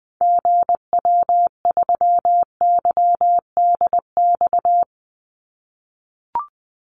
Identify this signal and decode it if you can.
Morse code, heard as a keyed tone, about 20 wpm, 700 Hz. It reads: GW3YDX
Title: Call Signs - 20wpm